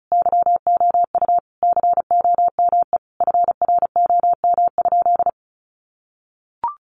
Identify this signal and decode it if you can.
Morse code, heard as a keyed tone, about 34 wpm, 700 Hz. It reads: YOU COME FROM?
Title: Sentences from Top 200 Words 35wpm